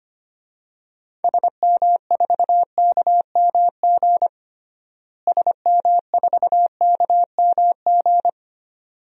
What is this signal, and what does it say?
Morse code, heard as a keyed tone, about 25 wpm, 700 Hz. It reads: SM4KMG SM4KMG
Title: DXCC-100 Call Signs - 6 Characters - Repeated 3x - 25wpm